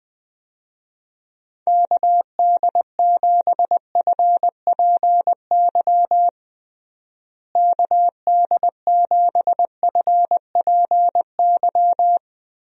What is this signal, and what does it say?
Morse code, heard as a keyed tone, about 20 wpm, 700 Hz. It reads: KD7FPY KD7FPY
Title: USA Call Signs - 6 Characters - Repeated 3x - 20wpm